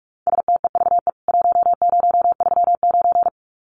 Morse code, heard as a keyed tone, about 44 wpm, 700 Hz. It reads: STEVE 1039
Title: CWT Exchange 45wpm